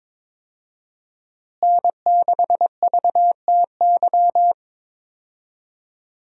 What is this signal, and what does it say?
Morse code, heard as a keyed tone, about 22 wpm, 700 Hz. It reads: N6VTY
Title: USA Call Signs - 5 Characters - Repeated 3x - 22wpm